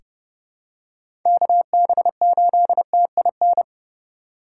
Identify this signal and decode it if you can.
Morse code, heard as a keyed tone, about 30 wpm, 700 Hz. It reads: KB8TIN
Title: USA Call Signs - 6 Characters - 2x Character Spacing - 30wpm